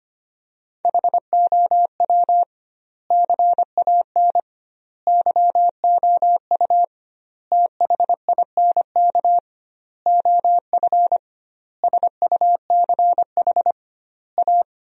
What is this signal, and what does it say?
Morse code, heard as a keyed tone, about 25 wpm, 700 Hz. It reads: HOW CAN YOU THINK OF SUCH A
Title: 2x Word-Spacing - Sentences from Top 200 Words - 25wpm